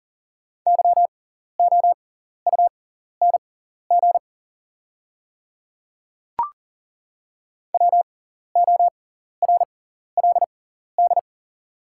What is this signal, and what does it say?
Morse code, heard as a keyed tone, about 40 wpm, spaced as for 14 wpm, 700 Hz. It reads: YOUNG WORLD
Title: 6x Character Spacing - Top 500 Words - 40wpm